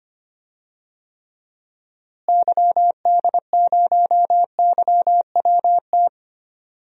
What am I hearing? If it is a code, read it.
Morse code, heard as a keyed tone, about 25 wpm, 700 Hz. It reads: YD0YWT
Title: DXCC-100 Call Signs - 6 Characters - Repeated 3x - 25wpm